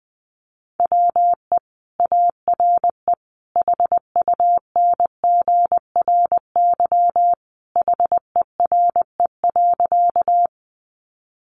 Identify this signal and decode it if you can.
Morse code, heard as a keyed tone, about 20 wpm, 700 Hz. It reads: WE ARE HUNGRY HERE.
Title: Sentences from Top 1400 Words 20wpm